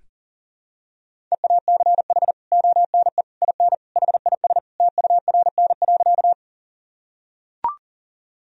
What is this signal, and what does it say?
Morse code, heard as a keyed tone, about 40 wpm, 700 Hz. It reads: EACH ONE IN HIS TURN.
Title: Sentences from Top 1000 Words - Review Part 1 - 40wpm